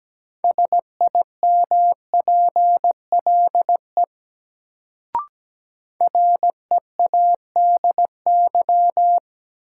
Morse code, heard as a keyed tone, about 17 wpm, 700 Hz. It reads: SIMPLE READY
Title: Sentences from Top 400 Words - New Word Drill - 17wpm